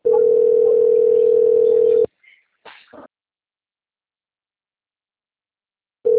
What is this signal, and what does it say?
Ringback tone (North America), 440 + 480 Hz.
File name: out- - - - - .29786